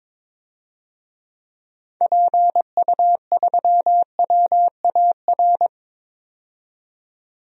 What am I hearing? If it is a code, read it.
Morse code, heard as a keyed tone, about 22 wpm, 700 Hz. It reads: PU3WAR
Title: DXCC-100 Call Signs - 6 Characters - 22wpm